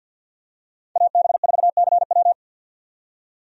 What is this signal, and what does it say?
Morse code, heard as a keyed tone, about 53 wpm, 700 Hz. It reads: AB4CW
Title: USA Call Signs - 5 Characters - 50wpm